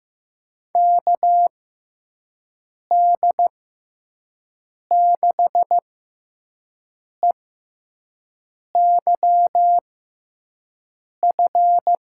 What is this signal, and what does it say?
Morse code, heard as a keyed tone, about 15 wpm, spaced as for 5 wpm, 700 Hz. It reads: KD6EYF
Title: USA Call Signs - 6 Characters - 6x Character Spacing - 15wpm